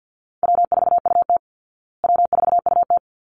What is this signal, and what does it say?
Morse code, heard as a keyed tone, about 49 wpm, 700 Hz. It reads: W4UT W4UT
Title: Call Signs Repeated 3x - 50wpm